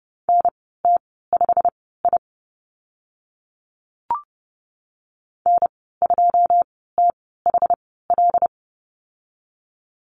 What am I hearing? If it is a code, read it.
Morse code, heard as a keyed tone, about 30 wpm, spaced as for 17 wpm, 700 Hz. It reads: NT5I N2THL
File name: Lesson 15g - Number 5 - Call Signs 30wpm